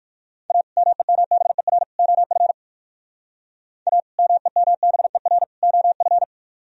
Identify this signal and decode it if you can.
Morse code, heard as a keyed tone, about 51 wpm, 700 Hz. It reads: A MEMBER OF A MEMBER OF
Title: Speed-Racing - Sets of 3 Words 45wpm